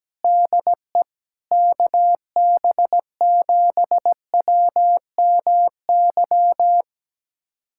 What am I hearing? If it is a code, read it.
Morse code, heard as a keyed tone, about 17 wpm, 700 Hz. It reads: DE KB7WMY
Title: USA Callsign DE Callsign - 6 Characters - 17wpm